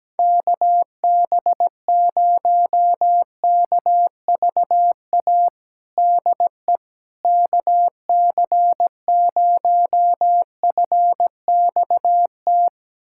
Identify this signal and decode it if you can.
Morse code, heard as a keyed tone, about 17 wpm, 700 Hz. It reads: KB0KVA DE KC0FXT
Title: USA Callsign DE Callsign - 6 Characters - 17wpm